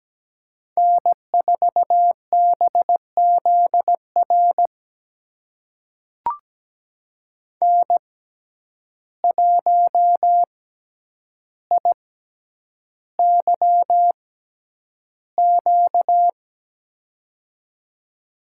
Morse code, heard as a keyed tone, about 17 wpm, spaced as for 9 wpm, 700 Hz. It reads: N4BZR N1IYQ
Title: USA Call Signs - 5 Characters - 6x Character Spacing - 17wpm